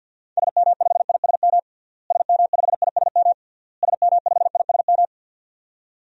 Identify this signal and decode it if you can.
Morse code, heard as a keyed tone, about 51 wpm, 700 Hz. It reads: SM5ISM SM5ISM SM5ISM
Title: DXCC-100 Call Signs - 6 Characters - Repeated 3x - 50wpm